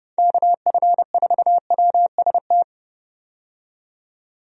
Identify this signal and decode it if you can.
Morse code, heard as a keyed tone, about 30 wpm, 700 Hz. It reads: KF4WST
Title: USA Call Signs - 6 Characters - 30wpm